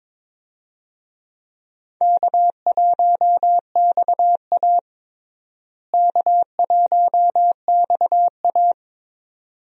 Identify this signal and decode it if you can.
Morse code, heard as a keyed tone, about 22 wpm, 700 Hz. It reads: K1XA K1XA
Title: USA Call Signs - 4 Characters - Repeated 2x - 22wpm